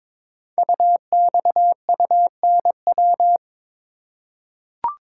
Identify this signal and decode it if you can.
Morse code, heard as a keyed tone, about 22 wpm, 700 Hz. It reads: UXUNW
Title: Code Groups - 5 Letters - 22wpm